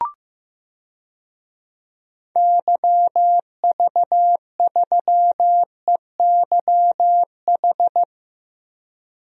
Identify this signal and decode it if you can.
Morse code, heard as a keyed tone, about 15 wpm, 700 Hz. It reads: YV3EYH
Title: DXCC-100 Call Signs - 6 Characters - Repeated 2x - 15wpm